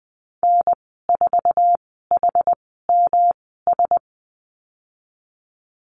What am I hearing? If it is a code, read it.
Morse code, heard as a keyed tone, about 20 wpm, spaced as for 15 wpm, 700 Hz. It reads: N4HMS
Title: 161 Lesson 26g - Letter g - Call Signs 20wpm